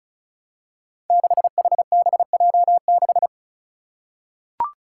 Morse code, heard as a keyed tone, about 36 wpm, 700 Hz. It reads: 6HBJ6
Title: Five Challenging Letter-Number Combinations 35wpm